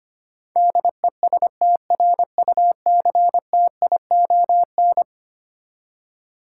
Morse code, heard as a keyed tone, about 25 wpm, 700 Hz. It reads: DESTRUCTION
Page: morse-code-ninja-practice-25wpm